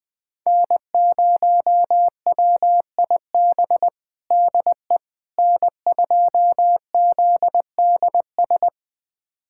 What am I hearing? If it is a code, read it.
Morse code, heard as a keyed tone, about 20 wpm, 700 Hz. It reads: N0WIB DE N2ZDS
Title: USA Callsign DE Callsign - 5 Characters - 20wpm